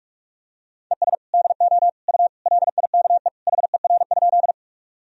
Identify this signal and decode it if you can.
Morse code, heard as a keyed tone, about 41 wpm, 700 Hz. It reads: ES DO U LIKE HER?
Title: Sentences from Top 100 Words and Abbreviations 45wpm